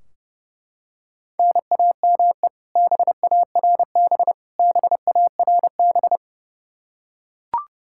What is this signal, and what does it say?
Morse code, heard as a keyed tone, about 29 wpm, 700 Hz. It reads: NAME BARB BARB
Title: Practice QSO - Exchange 1 - Beginning - 30wpm